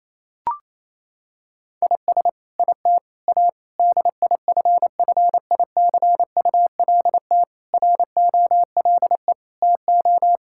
Morse code, heard as a keyed tone, about 28 wpm, 700 Hz. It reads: IS IT A DIFFICULT ROLE TO
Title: Sentences from Top 1000 Words - Review Part 6 - 28wpm